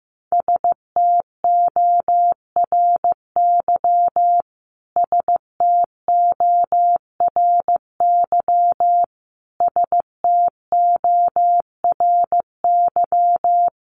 Morse code, heard as a keyed tone, about 15 wpm, 700 Hz. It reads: STORY STORY STORY
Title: Top 500 Words Repeated 15wpm